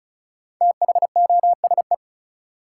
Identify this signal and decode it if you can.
Morse code, heard as a keyed tone, about 35 wpm, 700 Hz. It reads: THOSE
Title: Sentences from Top 300 Words - New Word Drill - 35wpm